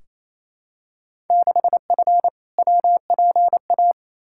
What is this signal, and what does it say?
Morse code, heard as a keyed tone, about 28 wpm, 700 Hz. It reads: 6F WPA
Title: Field Day Exchanges Encore - 28wpm